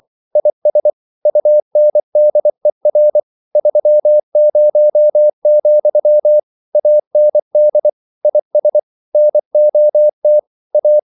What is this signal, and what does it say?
Morse code, heard as a keyed tone, about 24 wpm, 600 Hz. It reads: IS UNDER 30, AND IS NOT A
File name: CW / Practice / EnglishQuotations0224